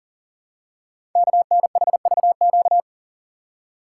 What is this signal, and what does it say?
Morse code, heard as a keyed tone, about 40 wpm, 700 Hz. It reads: KNHVQ
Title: Code Groups - 5 Letters - 40wpm